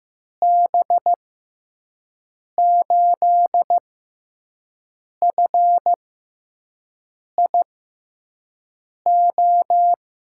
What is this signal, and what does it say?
Morse code, heard as a keyed tone, about 15 wpm, spaced as for 5 wpm, 700 Hz. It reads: B8FIO
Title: USA Call Signs - 6 Characters - 6x Character Spacing - 15wpm